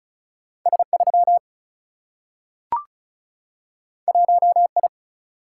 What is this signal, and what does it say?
Morse code, heard as a keyed tone, about 36 wpm, 700 Hz. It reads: S3 1I
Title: Single-Letter Single-Number 35wpm